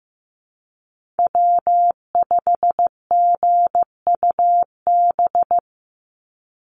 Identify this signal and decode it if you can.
Morse code, heard as a keyed tone, about 15 wpm, 700 Hz. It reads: W5GUB
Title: USA Call Signs 15wpm